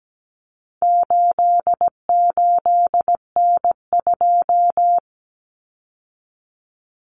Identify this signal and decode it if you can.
Morse code, heard as a keyed tone, about 17 wpm, 700 Hz. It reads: 88N2
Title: Common Cut Numbers - 4-Digit 17wpm